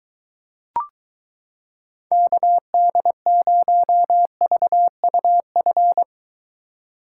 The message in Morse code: KD0VUF

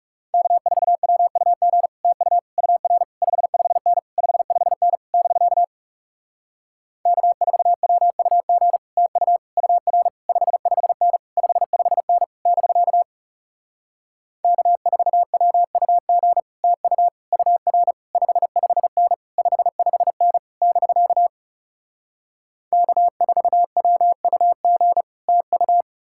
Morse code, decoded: K4WUG TU UR 55N 55N <BK> K4WUG TU UR 55N 55N <BK> K4WUG TU UR 55N 55N <BK> K4WUG TU

